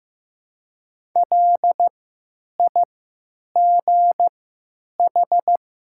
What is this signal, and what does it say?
Morse code, heard as a keyed tone, about 15 wpm, spaced as for 9 wpm, 700 Hz. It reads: LIGH
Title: 3x Character Spacing - Top 500 Words - 15wpm